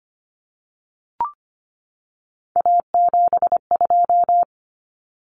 Morse code, heard as a keyed tone, about 25 wpm, 700 Hz. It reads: A72
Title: Common Cut Numbers - 3-Digit 25wpm